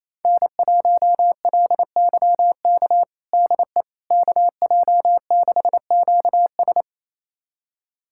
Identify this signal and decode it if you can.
Morse code, heard as a keyed tone, about 28 wpm, 700 Hz. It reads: N1LYK DE KJ6QS
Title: USA Callsign DE Callsign - 5 Characters - 28wpm